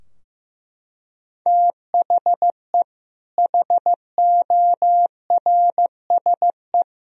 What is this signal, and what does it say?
Morse code, heard as a keyed tone, about 15 wpm, 700 Hz. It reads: THE HORSE
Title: Sentences from Top 1000 Words - Review Part 2 - 15wpm